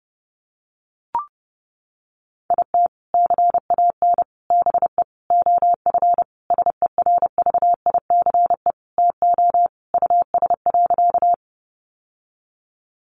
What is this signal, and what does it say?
Morse code, heard as a keyed tone, about 28 wpm, 700 Hz. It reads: IT CAN BE OF SERVICE TO US.
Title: Sentences from Top 1200 Words 30wpm